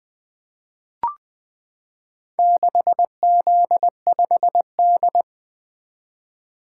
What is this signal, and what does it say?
Morse code, heard as a keyed tone, about 20 wpm, 700 Hz. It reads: 6Z5D
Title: Four Letter-Number Combinations - 20wpm